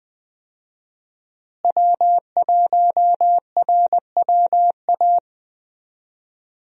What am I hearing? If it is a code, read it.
Morse code, heard as a keyed tone, about 20 wpm, 700 Hz. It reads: W1RWA
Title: USA Call Signs - 5 Characters - Repeated 2x - 20wpm